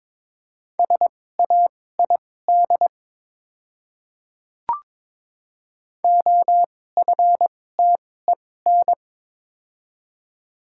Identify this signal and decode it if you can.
Morse code, heard as a keyed tone, about 22 wpm, spaced as for 16 wpm, 690 Hz. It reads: SAID OFTEN
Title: 2x Character Spacing - Top 300 Words - 22wpm